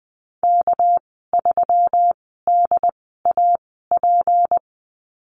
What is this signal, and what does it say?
Morse code, heard as a keyed tone, about 20 wpm, spaced as for 14 wpm, 700 Hz. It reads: K3DAP
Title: 200 Lesson 32g - Letter k - Call Signs 20wpm